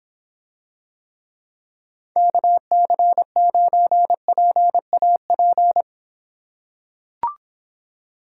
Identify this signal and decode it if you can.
Morse code, heard as a keyed tone, about 25 wpm, 700 Hz. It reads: KC9PAP